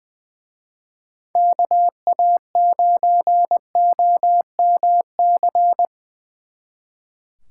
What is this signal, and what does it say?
Morse code, heard as a keyed tone, about 22 wpm, 700 Hz. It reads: KA9OMC
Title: Speed Racing - USA Call Signs 20wpm